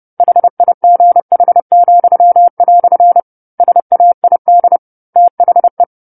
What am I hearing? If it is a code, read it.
Morse code, heard as a keyed tone, about 30 wpm, 700 Hz. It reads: HIGH," SAID THE
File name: 10-Chapter-10 30 WPM